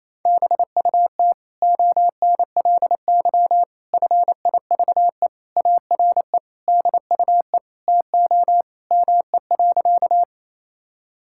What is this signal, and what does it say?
Morse code, heard as a keyed tone, about 28 wpm, 700 Hz. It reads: BUT ONLY FIVE ARE DUE TO ME.
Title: Sentences from Top 1000 Words - Review Part 6 - 28wpm